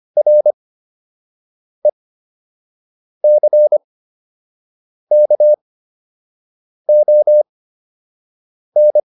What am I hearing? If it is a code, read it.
Morse code, heard as a keyed tone, about 25 wpm, spaced as for 6 wpm, 600 Hz. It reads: RECKON